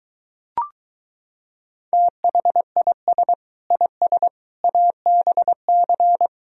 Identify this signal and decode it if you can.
Morse code, heard as a keyed tone, about 24 wpm, 700 Hz. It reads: THIS IS ABC